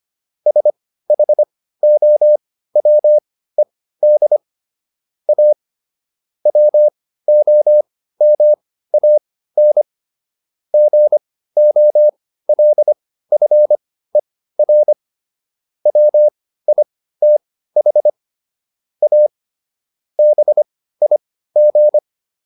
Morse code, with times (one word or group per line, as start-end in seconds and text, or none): 0.5-4.4 SHOWED
5.3-5.5 A
6.5-9.8 WOMAN
10.7-14.9 GOLFER
15.9-18.1 WITH
19.0-19.3 A
20.2-22.0 BIG